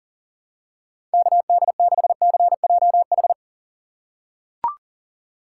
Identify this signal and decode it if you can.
Morse code, heard as a keyed tone, about 40 wpm, 700 Hz. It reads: KD6CJH
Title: USA Call Signs - 6 Characters - 2x Character Spacing - 40wpm